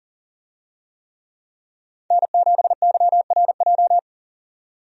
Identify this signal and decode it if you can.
Morse code, heard as a keyed tone, about 40 wpm, 700 Hz. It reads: N7YRJ
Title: USA Call Signs - 5 Characters - 40wpm